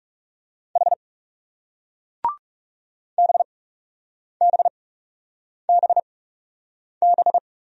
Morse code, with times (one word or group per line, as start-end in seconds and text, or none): 0.8-0.9 H
3.2-7.4 BBBB